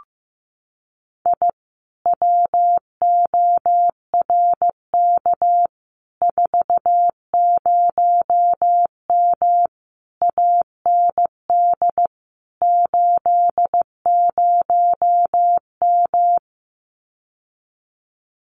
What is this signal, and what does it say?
Morse code, heard as a keyed tone, about 15 wpm, 700 Hz. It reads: I WORK 40M AND 80M
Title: Common QSO Phrases Encore 15wpm